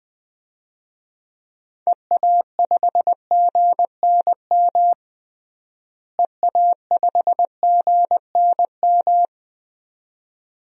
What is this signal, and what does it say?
Morse code, heard as a keyed tone, about 20 wpm, 700 Hz. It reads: EA5GNM EA5GNM
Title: DXCC-100 Call Signs - 6 Characters - Repeated 2x - 20wpm